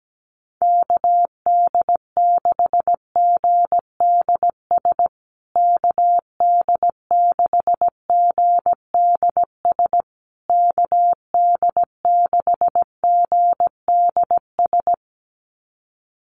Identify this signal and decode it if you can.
Morse code, heard as a keyed tone, about 17 wpm, 700 Hz. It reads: KD6GDS KD6GDS KD6GDS
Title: USA Call Signs Repeated 3x 17wpm